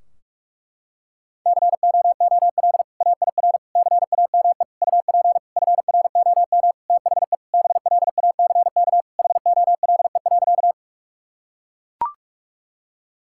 Morse code, heard as a keyed tone, about 41 wpm, 700 Hz. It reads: COOL AIR CAME UP FROM THE BLACK HOLE.
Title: Sentences from Top 1600 Words 45wpm